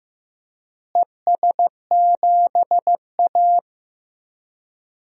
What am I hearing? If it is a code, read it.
Morse code, heard as a keyed tone, about 15 wpm, 700 Hz. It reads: ES7A
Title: DXCC-100 Call Signs - 4 Characters - Repeated 3x - 15wpm